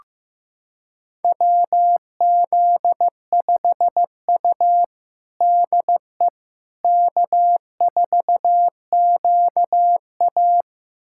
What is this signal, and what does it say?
Morse code, heard as a keyed tone, about 15 wpm, 700 Hz. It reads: WZ5U DE K4QA
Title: USA Callsign DE Callsign - 4 Characters - 15wpm